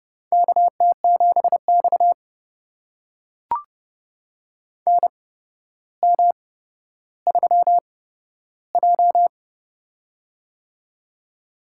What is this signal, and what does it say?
Morse code, heard as a keyed tone, about 32 wpm, spaced as for 13 wpm, 700 Hz. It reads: KT7X NM3J
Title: USA Call Signs - 4 Characters - 8x Character Spacing - 30wpm